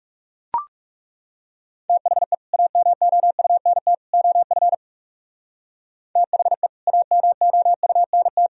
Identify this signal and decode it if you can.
Morse code, heard as a keyed tone, about 36 wpm, 700 Hz. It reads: THE AMOUNT OF THE AMOUNT
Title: Speed-Racing - Sets of 3 Words 30wpm